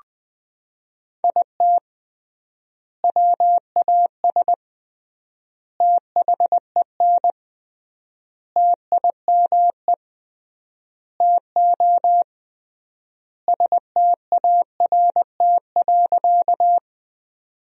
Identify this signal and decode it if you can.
Morse code, heard as a keyed tone, about 20 wpm, 700 Hz. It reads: IT WAS THEN TIME TO START.
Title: 3x Word-Spacing - Sentences from Top 300 Words - 20wpm